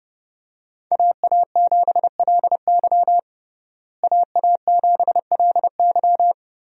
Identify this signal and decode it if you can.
Morse code, heard as a keyed tone, about 32 wpm, 700 Hz. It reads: AA7LY AA7LY
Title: USA Call Signs - 5 Characters - Repeated 3x - 30wpm